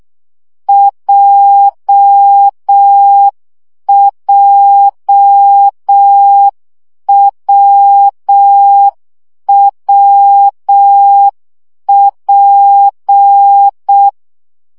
Morse code, heard as a keyed tone, about 6 wpm, 800 Hz. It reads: JJWWP
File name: CW gyakorlas w p j 5 baud